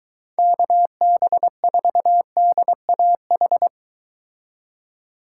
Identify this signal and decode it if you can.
Morse code, heard as a keyed tone, about 23 wpm, 700 Hz. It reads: KB4DAH